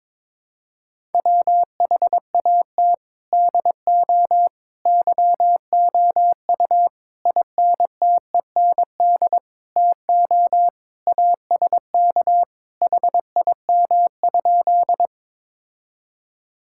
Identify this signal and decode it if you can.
Morse code, heard as a keyed tone, about 22 wpm, 700 Hz. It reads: WHAT DO YOU INTEND TO ASK HIM?
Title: Sentences from Top 1600 Words 22wpm